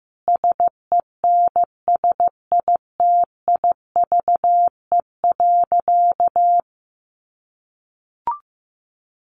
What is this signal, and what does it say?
Morse code, heard as a keyed tone, about 15 wpm, 700 Hz. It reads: SENSITIVE.
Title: Sentences from Top 1400 Words 15wpm